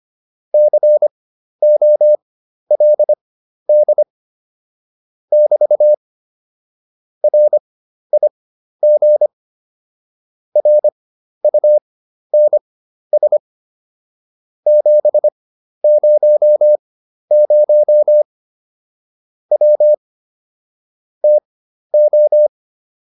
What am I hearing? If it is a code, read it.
Morse code, heard as a keyed tone, about 25 wpm, spaced as for 12 wpm, 600 Hz. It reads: COLD <BT> RIG RUNS 700 W TO